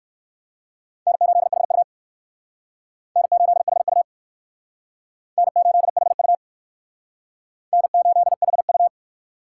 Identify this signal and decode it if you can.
Morse code, heard as a keyed tone, about 61 wpm, 700 Hz. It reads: N8HU N8HU N8HU N8HU